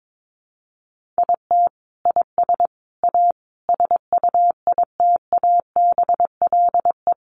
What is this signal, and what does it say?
Morse code, heard as a keyed tone, about 22 wpm, 700 Hz. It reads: IT IS A SUITABLE
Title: Sentences from Top 1400 Words 22wpm